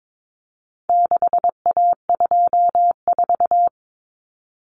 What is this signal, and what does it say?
Morse code, heard as a keyed tone, about 22 wpm, 700 Hz. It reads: 6A24
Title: Common Cut Numbers - 4-Digit 22wpm